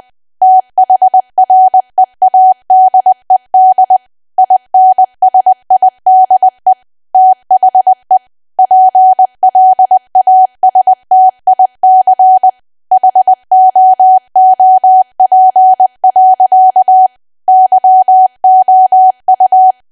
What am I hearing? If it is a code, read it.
Morse code, heard as a keyed tone, about 20 wpm, 750 Hz. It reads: THREADED INSIDE THE PLASTIC HOOP. YOU
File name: cwa / W1AW-Level3 / 20WPM